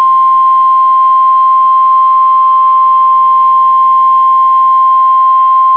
Emergency Alert System (SAME) tones (the 1050 Hz attention tone). It is heard from 0.8 to 5.3 s.